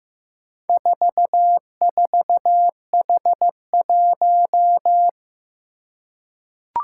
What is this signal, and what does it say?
Morse code, heard as a keyed tone, about 15 wpm, 700 Hz. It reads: 44H1
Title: Four Challenging Letter-Number Combinations 15wpm